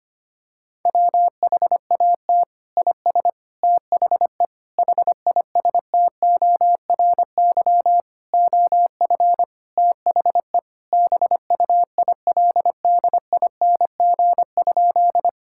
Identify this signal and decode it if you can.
Morse code, heard as a keyed tone, about 25 wpm, 700 Hz. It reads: WHAT IS THE HISTORY OF THE BUILDING?
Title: Sentences from Top 1000 Words - Review Part 5 - 25wpm